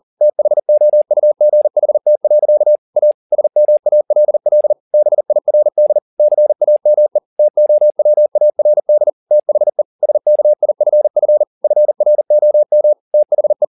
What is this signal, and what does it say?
Morse code, heard as a keyed tone, about 40 wpm, 600 Hz. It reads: THOUGHT. A SMALL BIRD CAME TOWARD THE SKIFF FROM THE